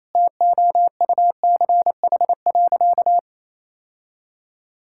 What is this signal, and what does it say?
Morse code, heard as a keyed tone, about 28 wpm, 700 Hz. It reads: TOUCH.
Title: Sentences from Top 1600 Words 28wpm